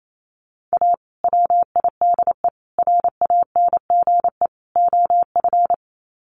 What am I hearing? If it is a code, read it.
Morse code, heard as a keyed tone, about 28 wpm, 700 Hz. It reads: A WIDE RANGE OF
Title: Sets of 4 Words 28wpm